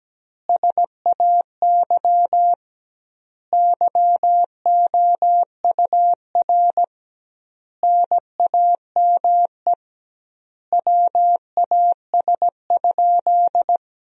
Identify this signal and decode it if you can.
Morse code, heard as a keyed tone, about 17 wpm, 700 Hz. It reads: SAY YOUR NAME WAS?
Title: 2x Word-Spacing - Sentences from Top 200 Words - 17wpm